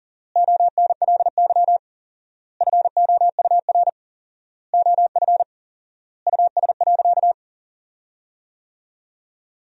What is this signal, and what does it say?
Morse code, heard as a keyed tone, about 40 wpm, 700 Hz. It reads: ONLY FOUR OF US.
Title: 4x Word-Spacing - Sentences from Top 300 Words - 40wpm